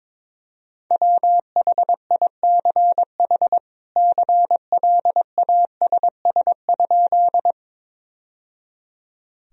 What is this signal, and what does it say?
Morse code, heard as a keyed tone, about 22 wpm, 700 Hz. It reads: WHICH CLASS?
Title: Sentences from Top 1000 Words - Review Part 2 - 22wpm